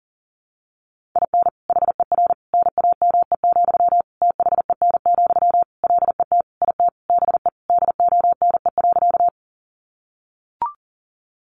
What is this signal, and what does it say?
Morse code, heard as a keyed tone, about 40 wpm, 700 Hz. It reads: IN HER NAME, THEN, LET IT BE DONE.
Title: Sentences from Top 500 Words 40wpm